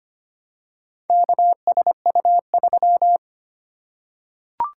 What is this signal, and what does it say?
Morse code, heard as a keyed tone, about 25 wpm, 700 Hz. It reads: KSU3
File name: practice / 25wpm / Four Letter-Number Combinations - 25wpm